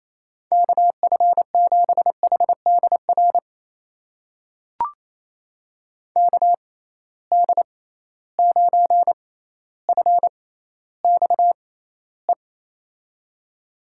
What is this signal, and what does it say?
Morse code, heard as a keyed tone, about 28 wpm, spaced as for 14 wpm, 700 Hz. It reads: KF7HDR KD9FXE
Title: USA Call Signs - 6 Characters - 6x Character Spacing - 28wpm